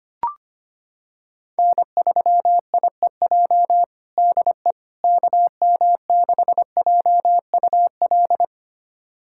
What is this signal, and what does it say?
Morse code, heard as a keyed tone, about 25 wpm, 700 Hz. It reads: N3IEJ DE KM6JUL
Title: USA Callsign DE Callsign 25wpm